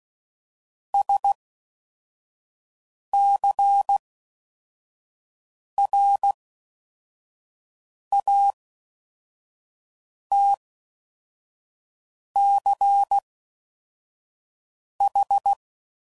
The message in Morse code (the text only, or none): SCRATCH